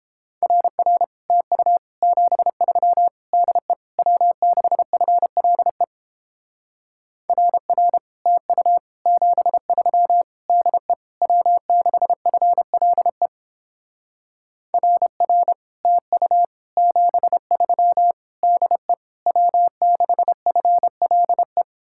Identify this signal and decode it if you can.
Morse code, heard as a keyed tone, about 29 wpm, 700 Hz. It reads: RR TU 73 DE W6FLE RR TU 73 DE W6FLE RR TU 73 DE W6FLE